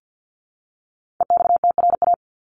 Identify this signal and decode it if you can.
Morse code, heard as a keyed tone, about 48 wpm, 700 Hz. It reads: EXTRA
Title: Top 751-1000 Words 50wpm